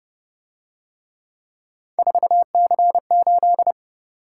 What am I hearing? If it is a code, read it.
Morse code, heard as a keyed tone, about 30 wpm, 700 Hz. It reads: 4C8